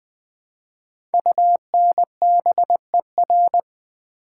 Morse code, heard as a keyed tone, about 20 wpm, 700 Hz. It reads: UNBER
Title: Code Groups - 5 Letters - 20wpm